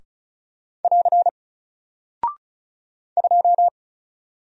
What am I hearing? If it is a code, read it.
Morse code, heard as a keyed tone, about 36 wpm, 700 Hz. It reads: <AR> 2
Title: Single Character (Letter:Number:Punctuation:Prosign) Alt 35wpm